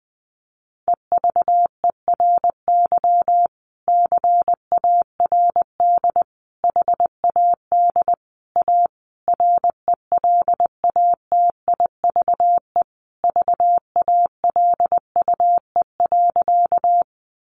Morse code, heard as a keyed tone, about 19 wpm, 700 Hz. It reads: EVERY CARD HAD A RELATIVE VALUE.
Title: Sentences from Top 1400 Words 20wpm